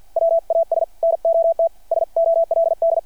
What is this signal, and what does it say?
Morse code, heard as a keyed tone, about 52 wpm, 660 Hz. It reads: WAS NOT SOLD
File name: archived / CRSnet